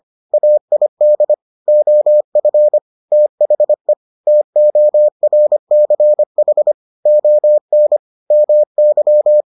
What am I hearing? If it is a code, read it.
Morse code, heard as a keyed tone, about 25 wpm, 600 Hz. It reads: AID OF THE TORCH ON MY